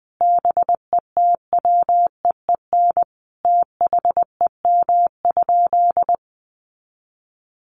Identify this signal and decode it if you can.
Morse code, heard as a keyed tone, about 20 wpm, 700 Hz. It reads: BETWEEN THEM?
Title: Sentences from Top 1300 Words 20wpm